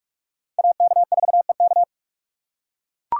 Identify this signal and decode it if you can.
Morse code, heard as a keyed tone, about 43 wpm, 700 Hz. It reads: AK4EK
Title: USA Call Signs - 5 Characters - 4x Character Spacing - 45wpm